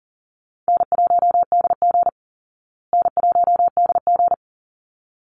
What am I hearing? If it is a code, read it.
Morse code, heard as a keyed tone, about 40 wpm, 700 Hz. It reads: N1DG N1DG
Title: Easy Call Signs Repeated 2x - 40wpm